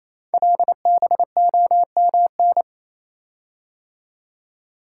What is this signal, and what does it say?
Morse code, heard as a keyed tone, about 28 wpm, 700 Hz. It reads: LBOMN
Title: Code Groups - 5 Letters - 28wpm